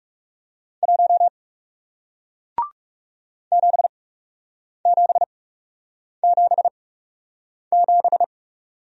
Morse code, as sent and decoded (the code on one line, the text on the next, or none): .---- / --... --... --... --...
1 7777